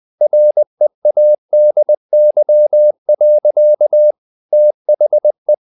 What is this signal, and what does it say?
Morse code, heard as a keyed tone, about 20 wpm, 600 Hz. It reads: READY. THE
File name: the old man and the sea / the old man and the sea part 1 20wpm